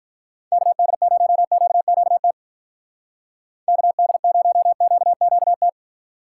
Morse code, heard as a keyed tone, about 43 wpm, 700 Hz. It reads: KD0QQT KD0QQT
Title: Speed Racing - USA 2x3 Call Signs 35wpm